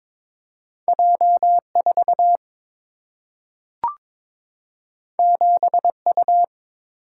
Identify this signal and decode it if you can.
Morse code, heard as a keyed tone, about 22 wpm, 700 Hz. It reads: J 4 7 U
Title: Single-Letter Single-Number 22wpm